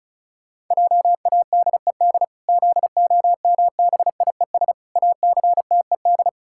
Morse code, heard as a keyed tone, about 34 wpm, 700 Hz. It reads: JADED ZOMBIES ACTED
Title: Pangram Challenge v2 - 35wpm